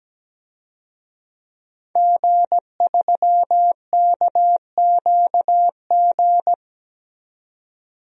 Morse code, heard as a keyed tone, about 17 wpm, 700 Hz. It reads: G3KQG
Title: DXCC-100 Call Signs - 5 Characters - 17wpm